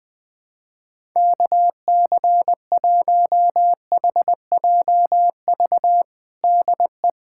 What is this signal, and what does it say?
Morse code, heard as a keyed tone, about 20 wpm, 700 Hz. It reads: KC1HJV DE
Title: USA Callsign DE Callsign 20wpm